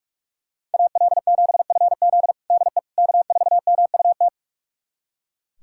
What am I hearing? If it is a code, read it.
Morse code, heard as a keyed tone, about 43 wpm, 700 Hz. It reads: AL7FZ DE K4MUT
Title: USA Callsign DE Callsign - 5 Characters - 45wpm